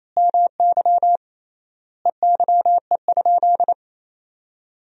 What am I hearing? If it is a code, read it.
Morse code, heard as a keyed tone, about 27 wpm, 700 Hz. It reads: MY EYE?
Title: 3x Word-Spacing - Sentences from Top 500 Words - 28wpm